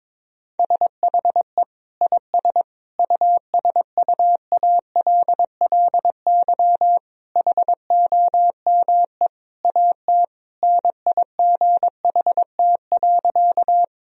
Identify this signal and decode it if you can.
Morse code, heard as a keyed tone, about 22 wpm, 700 Hz. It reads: SHE IS USUALLY HOME AT NIGHT.
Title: Sentences from Top 1000 Words - Review Part 3 - 22wpm